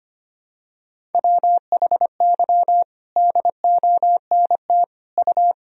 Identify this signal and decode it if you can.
Morse code, heard as a keyed tone, about 25 wpm, 700 Hz. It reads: WHY DONT U
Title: Sentences from Top 300 Words and Abbreviations 25wpm